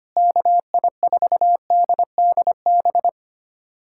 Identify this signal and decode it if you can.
Morse code, heard as a keyed tone, about 25 wpm, 700 Hz. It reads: KI4DDB
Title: USA Callsign DE Callsign - 6 Characters - 25wpm